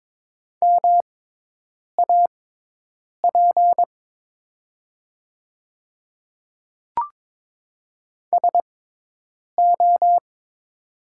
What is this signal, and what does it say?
Morse code, heard as a keyed tone, about 22 wpm, spaced as for 8 wpm, 700 Hz. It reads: MAP SO
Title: 6x Character Spacing - Top 500 Words - 22wpm